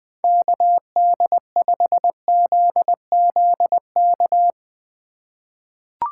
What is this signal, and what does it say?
Morse code, heard as a keyed tone, about 20 wpm, 700 Hz. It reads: KD5ZZK
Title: 2023 SOTA Activator Call Signs - 20wpm